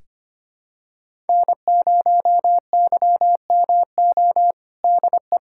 Morse code, heard as a keyed tone, about 25 wpm, 700 Hz. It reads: N0YMO DE
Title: USA Callsign DE Callsign - 5 Characters - 25wpm